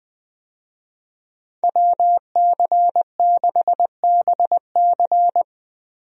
Morse code, heard as a keyed tone, about 21 wpm, 700 Hz. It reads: WC6BC
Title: USA Call Signs - 5 Characters - Repeated 3x - 20wpm